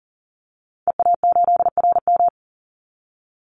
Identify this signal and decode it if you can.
Morse code, heard as a keyed tone, about 40 wpm, 700 Hz. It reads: EA8RM
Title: Call Signs - 40wpm